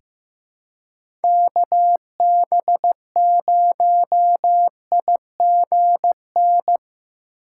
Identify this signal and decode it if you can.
Morse code, heard as a keyed tone, about 15 wpm, 700 Hz. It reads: KB0IGN
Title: USA Call Signs - 3x Character Spacing - 15wpm